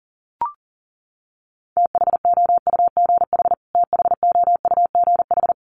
Morse code, heard as a keyed tone, about 40 wpm, 700 Hz. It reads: THOUGH THOUGH
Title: Top 500 Words Repeated 40wpm